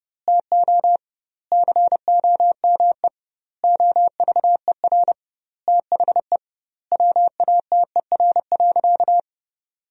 Morse code, heard as a keyed tone, about 30 wpm, 700 Hz. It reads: TO COME OVER THE WATER.
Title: 2x Word-Spacing - Sentences from Top 300 Words - 30wpm